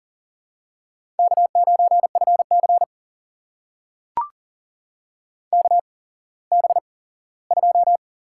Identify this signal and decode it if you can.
Morse code, heard as a keyed tone, about 38 wpm, spaced as for 19 wpm, 700 Hz. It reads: K9FC KB2
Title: USA Call Signs - 8x Character Spacing - 40wpm